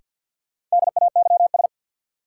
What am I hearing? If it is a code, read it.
Morse code, heard as a keyed tone, about 53 wpm, 700 Hz. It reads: DAYS
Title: Sentences from Top 1000 Words - Review Part 5 - New Word Drill - 50wpm